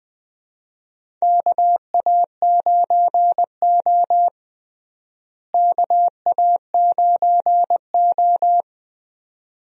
Morse code, heard as a keyed tone, about 21 wpm, 700 Hz. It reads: KA9O KA9O
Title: USA Call Signs - 4 Characters - Repeated 2x - 20wpm